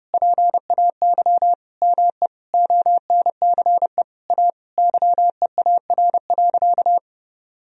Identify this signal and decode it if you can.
Morse code, heard as a keyed tone, about 30 wpm, 700 Hz. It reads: PAY ME ONCE A YEAR.
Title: Sentences from Top 1000 Words - Review Part 1 - 30wpm